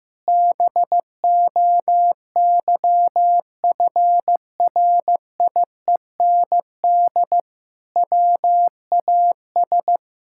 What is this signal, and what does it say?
Morse code, heard as a keyed tone, about 16 wpm, 700 Hz. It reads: BOYFRIEND WAS
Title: Sentences from Top 1000 Words - Review Part 5 - 15wpm